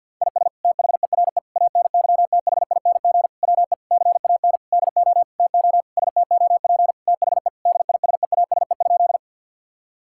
Morse code, heard as a keyed tone, about 50 wpm, 700 Hz. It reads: IS THERE ANYTHING WE CAN DO TO STOP THE DISEASE?
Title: Sentences from Top 1000 Words - Review Part 6 - 50wpm